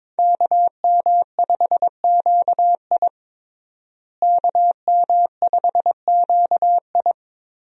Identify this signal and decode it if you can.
Morse code, heard as a keyed tone, about 22 wpm, 700 Hz. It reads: KM5QI KM5QI
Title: USA Call Signs - 5 Characters - Repeated 3x - 22wpm